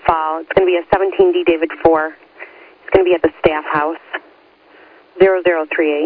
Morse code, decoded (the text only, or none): none